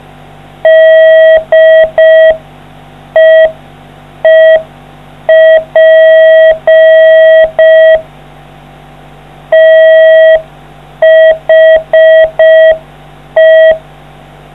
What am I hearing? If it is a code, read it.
Morse code, heard as a keyed tone, about 5 wpm, 640 Hz. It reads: DEEP THE